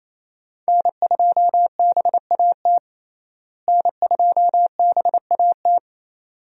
Morse code, heard as a keyed tone, about 28 wpm, 700 Hz. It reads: N2BAT N2BAT
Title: USA Call Signs - 5 Characters - Repeated 2x - 28wpm